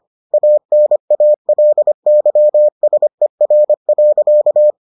Morse code, heard as a keyed tone, about 25 wpm, 600 Hz. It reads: ANALYSER.